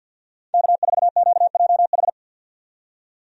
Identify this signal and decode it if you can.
Morse code, heard as a keyed tone, about 50 wpm, 700 Hz. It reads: K4QJH
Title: USA Call Signs - 5 Characters - 50wpm